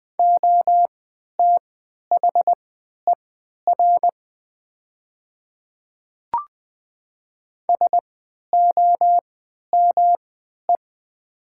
Morse code, heard as a keyed tone, about 20 wpm, spaced as for 11 wpm, 700 Hz. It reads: OTHER SOME
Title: 3x Character Spacing - Top 100 Words - 20wpm